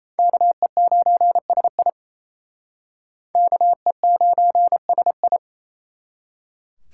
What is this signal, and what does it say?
Morse code, heard as a keyed tone, about 34 wpm, 700 Hz. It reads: KE9SI KE9SI
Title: Speed Racing - USA 2x2 Call Signs 28wpm